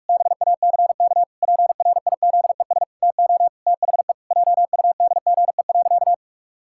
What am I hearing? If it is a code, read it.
Morse code, heard as a keyed tone, about 46 wpm, 700 Hz. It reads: BACK PRIZES TO THE JUDGE.
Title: Pangram Challenge v2 - 45wpm